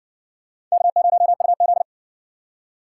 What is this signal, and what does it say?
Morse code, heard as a keyed tone, about 57 wpm, 700 Hz. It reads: K0UZ